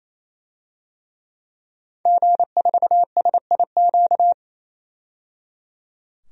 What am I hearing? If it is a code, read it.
Morse code, heard as a keyed tone, about 27 wpm, 700 Hz. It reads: G4SIQ
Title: DXCC-100 Call Signs - 5 Characters - 28wpm